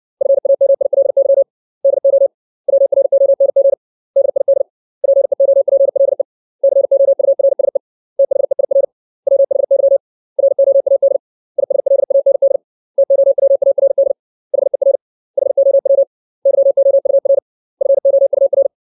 Morse code, heard as a keyed tone, about 60 wpm, 550 Hz. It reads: FAMILY DO WRONG DID PEOPLE COURSE THEIR WHY ROAD ISLAND TOWARD HER HOW YOUR FOUR
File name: 632 words / 632 Words 60WPM